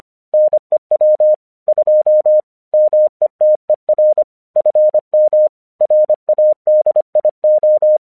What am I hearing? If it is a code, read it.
Morse code, heard as a keyed tone, about 25 wpm, 610 Hz. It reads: NEW 2 METER FM RADIO
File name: 1 English 25wpm / AQC14